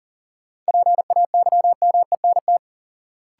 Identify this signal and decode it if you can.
Morse code, heard as a keyed tone, about 40 wpm, 700 Hz. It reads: PAYMENT
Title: Sentences from Top 1200 Words - New Word Drill - 40wpm